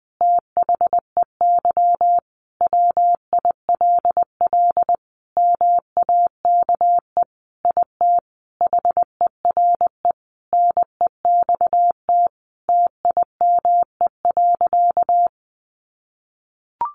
Morse code, heard as a keyed tone, about 20 wpm, 700 Hz. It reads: THEY WILL MAKE IT HERE NEXT TIME.
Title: Sentences from Top 1500 Words 20wpm